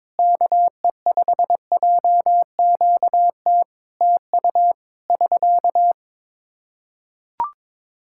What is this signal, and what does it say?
Morse code, heard as a keyed tone, about 22 wpm, 700 Hz. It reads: KE5JQT TU <SK>
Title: Practice QSO - Exchange 4 - Ending - 22wpm